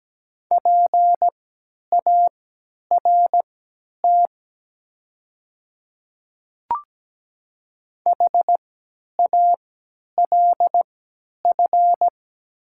Morse code, heard as a keyed tone, about 17 wpm, spaced as for 10 wpm, 700 Hz. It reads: PART HALF
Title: 3x Character Spacing - Top 500 Words - 17wpm